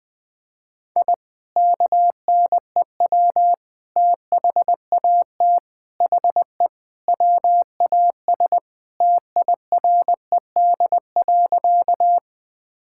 Morse code, heard as a keyed tone, about 21 wpm, 700 Hz. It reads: I KNEW THAT HE WAS TIRED.
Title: Sentences with 3-Word Phrases - 20wpm